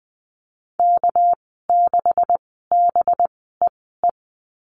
Morse code, heard as a keyed tone, about 20 wpm, spaced as for 14 wpm, 700 Hz. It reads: K6BEE
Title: 231 Lesson 37g - Letter x - Call Signs 20wpm